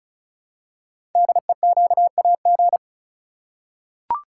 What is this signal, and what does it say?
Morse code, heard as a keyed tone, about 34 wpm, 700 Hz. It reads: DEQAG
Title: Code Groups - 5 Letters - 35wpm